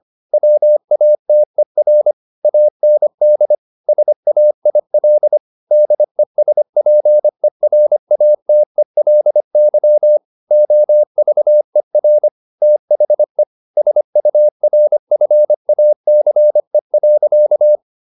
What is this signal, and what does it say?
Morse code, heard as a keyed tone, about 25 wpm, 600 Hz. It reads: WATER AND SAIL DESPERATELY OVER THE SURFACE.
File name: the old man and the sea part 1 25wpm